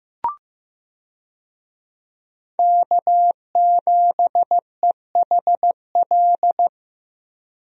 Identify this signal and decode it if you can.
Morse code, heard as a keyed tone, about 15 wpm, 700 Hz. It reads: K7EHL
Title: USA Call Signs - 5 Characters - 15wpm